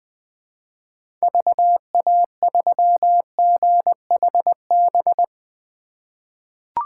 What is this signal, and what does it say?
Morse code, heard as a keyed tone, about 20 wpm, 700 Hz. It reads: VA3GHB
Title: 2023 POTA Activator Call Signs - 20wpm